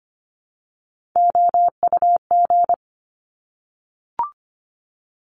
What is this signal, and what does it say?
Morse code, heard as a keyed tone, about 25 wpm, 700 Hz. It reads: OUG
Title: Top Three Letter Combinations in Words 25wpm